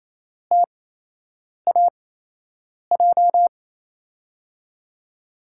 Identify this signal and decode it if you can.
Morse code, heard as a keyed tone, about 28 wpm, spaced as for 8 wpm, 700 Hz. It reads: TAJ
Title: USA Call Signs - 6 Characters - 8x Character Spacing - 28wpm